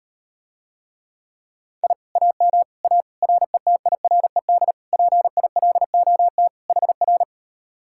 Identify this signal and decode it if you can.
Morse code, heard as a keyed tone, about 38 wpm, 700 Hz. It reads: I AM A RETIRED PILOT HR